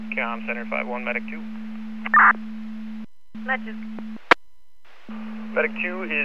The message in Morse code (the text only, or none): none